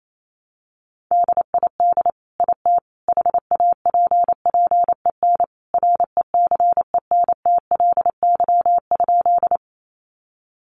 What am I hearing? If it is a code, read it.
Morse code, heard as a keyed tone, about 28 wpm, 700 Hz. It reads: DID IT HAPPEN RECENTLY?
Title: Sentences from Top 1200 Words 28wpm